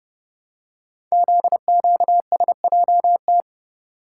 Morse code, heard as a keyed tone, about 29 wpm, 700 Hz. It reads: ZQSJT
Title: Code Groups - 5 Letters - 30wpm